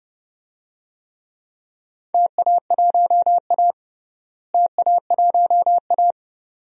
Morse code, heard as a keyed tone, about 30 wpm, 700 Hz. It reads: TA1A TA1A
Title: DXCC-100 Call Signs - 4 Characters - Repeated 2x - 30wpm